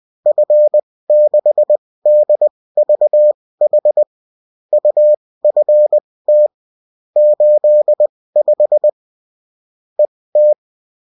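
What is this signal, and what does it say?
Morse code, heard as a keyed tone, about 20 wpm, spaced as for 16 wpm, 600 Hz. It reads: F6DVH UFT 85 ET
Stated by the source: Bulletin in text and CW form.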